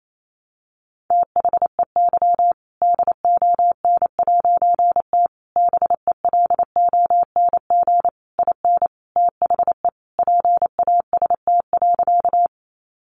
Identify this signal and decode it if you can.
Morse code, heard as a keyed tone, about 28 wpm, 700 Hz. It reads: THEY DON'T BELONG IN THE PAST.
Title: A Sound Of Thunder 28wpm